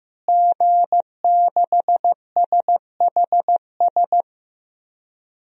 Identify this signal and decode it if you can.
Morse code, heard as a keyed tone, about 15 wpm, 700 Hz. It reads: G6SHS
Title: DXCC-100 Call Signs - 5 Characters - Repeated 3x - 15wpm